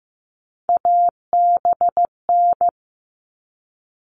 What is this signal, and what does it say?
Morse code, heard as a keyed tone, about 15 wpm, 700 Hz. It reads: ABN
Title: All Cut Numbers - 3-Digit 15wpm